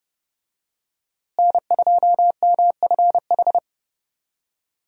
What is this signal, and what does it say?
Morse code, heard as a keyed tone, about 32 wpm, 700 Hz. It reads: N2MFH
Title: USA Call Signs - 5 Characters - Repeated 2x - 30wpm